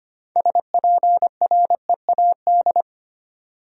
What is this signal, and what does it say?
Morse code, heard as a keyed tone, about 25 wpm, 700 Hz. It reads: SPREAD
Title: Sentences from Top 1200 Words - New Word Drill Alt - 25wpm